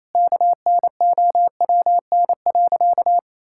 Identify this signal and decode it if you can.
Morse code, heard as a keyed tone, about 28 wpm, 700 Hz. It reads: KNOWN.
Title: Sentences from Top 1000 Words - Review Part 6 - 28wpm